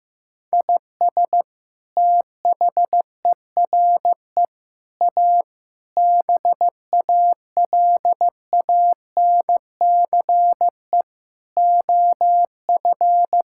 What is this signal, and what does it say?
Morse code, heard as a keyed tone, about 16 wpm, 700 Hz. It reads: IS THERE A BALANCE OF
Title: Sentences from Top 1000 Words - Review Part 6 - 15wpm